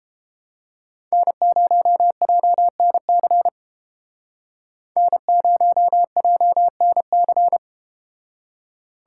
Morse code, heard as a keyed tone, about 32 wpm, 700 Hz. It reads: N0JNC N0JNC